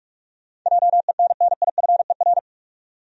Morse code, heard as a keyed tone, about 46 wpm, 700 Hz. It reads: JENNIFER
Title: USA Common Names - 45wpm